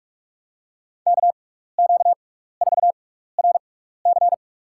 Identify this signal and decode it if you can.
Morse code, heard as a keyed tone, about 46 wpm, spaced as for 16 wpm, 690 Hz. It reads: KQ4RC
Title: USA Call Signs - 5 Characters - 6x Character Spacing - 45wpm